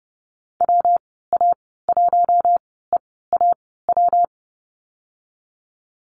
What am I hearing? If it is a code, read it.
Morse code, heard as a keyed tone, about 31 wpm, spaced as for 17 wpm, 700 Hz. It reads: WA1EAW
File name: Lesson 37g - Letter x - Call Signs 30wpm